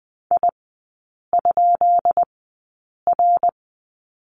Morse code, heard as a keyed tone, about 20 wpm, spaced as for 8 wpm, 700 Hz. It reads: I?R
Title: 133 Lesson 22e - Question Mark - Three Characters 20wpm